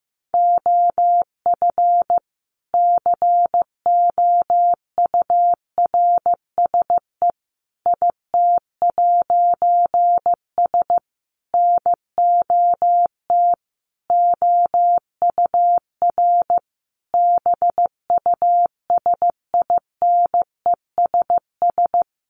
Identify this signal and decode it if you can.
Morse code, heard as a keyed tone, about 15 wpm, 700 Hz. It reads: OF COURSE IT'S NOT OUR BUSINESS
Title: A Sound Of Thunder 15wpm